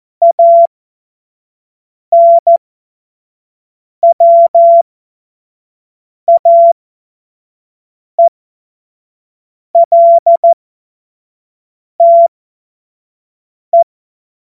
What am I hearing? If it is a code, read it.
Morse code, heard as a keyed tone, about 14 wpm, spaced as for 5 wpm, 680 Hz. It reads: ANWAELTE
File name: Morsekurs / Witze / Witz 31. Mai 2022 25 F70